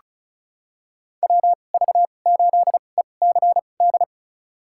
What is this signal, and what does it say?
Morse code, heard as a keyed tone, about 34 wpm, spaced as for 25 wpm, 700 Hz. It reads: WV8ECD
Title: USA Call Signs - 6 Characters - 2x Character Spacing - 35wpm